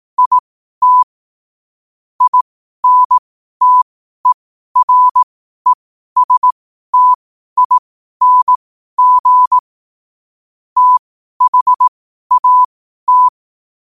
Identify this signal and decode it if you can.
Morse code, heard as a keyed tone, about 18 wpm, spaced as for 12 wpm, 1000 Hz. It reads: IT INTERESTING THAT
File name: qotdfiles / 12wpm